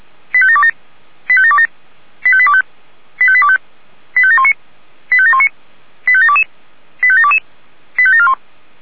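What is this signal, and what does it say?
Five-tone selective call (ZVEI), heard as quick runs of five tones, tones of about 70 ms. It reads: 85727 85727 85825 85825 85719 85719 85722 85722 85741
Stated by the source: Five-tone ZVEI selective call received on SDR.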